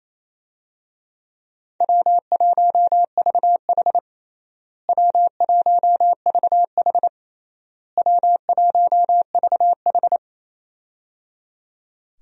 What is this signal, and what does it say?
Morse code, heard as a keyed tone, about 27 wpm, 700 Hz. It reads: W1VH W1VH W1VH
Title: USA Call Signs - 4 Characters - Repeated 3x - 28wpm